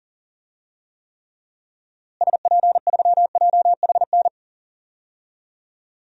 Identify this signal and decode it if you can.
Morse code, heard as a keyed tone, about 40 wpm, 700 Hz. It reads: SP3JHN
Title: DXCC-100 Call Signs - 6 Characters - Repeated 2x - 40wpm